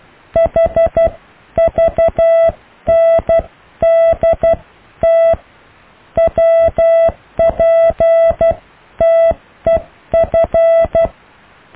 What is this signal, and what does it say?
Morse code, heard as a keyed tone, about 12 wpm, spaced as for 10 wpm, 660 Hz. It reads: HVNDT WPTEF
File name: cw / EKTPV+BFHNW